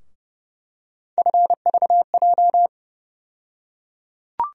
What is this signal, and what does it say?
Morse code, heard as a keyed tone, about 30 wpm, 700 Hz. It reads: FVJ
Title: USA Call Sign Postfixes - 3 Letters 30wpm